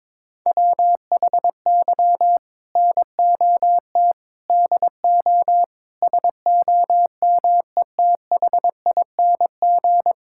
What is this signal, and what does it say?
Morse code, heard as a keyed tone, about 22 wpm, 700 Hz. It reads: WHY NOT DO SOMETHING
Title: Sentences from Top 1000 Words - Review Part 6 - 22wpm